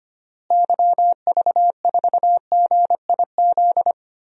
Morse code, heard as a keyed tone, about 25 wpm, 700 Hz. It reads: YV4GIZ
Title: DXCC-100 Call Signs - 6 Characters - Repeated 2x - 25wpm